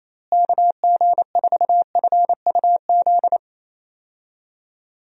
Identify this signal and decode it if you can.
Morse code, heard as a keyed tone, about 28 wpm, 700 Hz. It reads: KG4FUZ
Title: USA Call Signs - 6 Characters - Repeated 2x - 28wpm